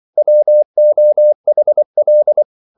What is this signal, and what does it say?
Morse code, heard as a keyed tone, about 24 wpm, 600 Hz. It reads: WOHL